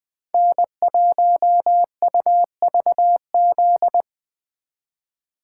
Morse code, heard as a keyed tone, about 20 wpm, 700 Hz. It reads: N1UVZ
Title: USA Call Signs - 2x Character Spacing - 20wpm